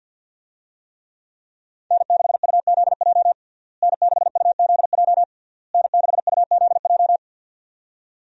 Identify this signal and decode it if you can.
Morse code, heard as a keyed tone, about 47 wpm, 700 Hz. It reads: N6UZJ N6UZJ N6UZJ
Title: USA Call Signs - 5 Characters - Repeated 3x - 50wpm